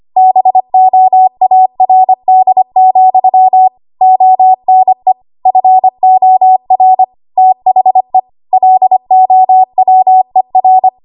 Morse code, heard as a keyed tone, about 25 wpm, 750 Hz. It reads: BOARD, ONE FOR THE LOWER
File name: Morse / 25WPM